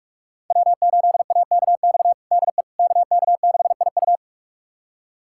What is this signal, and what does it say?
Morse code, heard as a keyed tone, about 46 wpm, 700 Hz. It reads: W8AKX DE KK6IU
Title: USA Callsign DE Callsign - 5 Characters - 45wpm